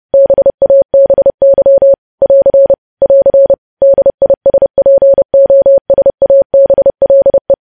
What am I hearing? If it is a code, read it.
Morse code, heard as a keyed tone, about 30 wpm, 550 Hz. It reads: BABY <AR> <AR> DISPOSABLE
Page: news headlines in morse code at 30 wpm